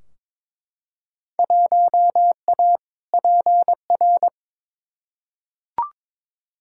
Morse code, heard as a keyed tone, about 21 wpm, 700 Hz. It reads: 1A PR
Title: Field Day Exchanges - 22wpm